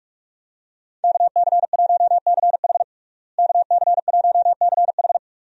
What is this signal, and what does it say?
Morse code, heard as a keyed tone, about 46 wpm, 700 Hz. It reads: KC1CH KC1CH
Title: USA Call Signs - 5 Characters - Repeated 3x - 45wpm